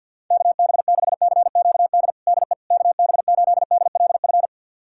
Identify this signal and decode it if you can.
Morse code, heard as a keyed tone, about 48 wpm, 700 Hz. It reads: KB6CQD DE KB8DLF
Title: USA Callsign DE Callsign 50wpm